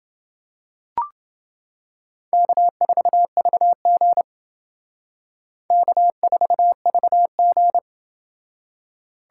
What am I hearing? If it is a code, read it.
Morse code, heard as a keyed tone, about 28 wpm, 700 Hz. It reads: K4VG K4VG